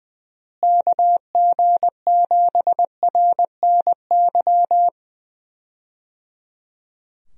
Morse code, heard as a keyed tone, about 21 wpm, 700 Hz. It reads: KG7RNY
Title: USA Call Signs - 6 Characters - Repeated 3x - 20wpm